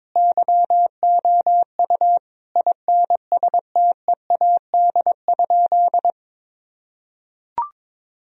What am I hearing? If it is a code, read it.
Morse code, heard as a keyed tone, about 22 wpm, 700 Hz. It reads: YOU INSTEAD?
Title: Sentences from Top 1000 Words - Review Part 4 - 22wpm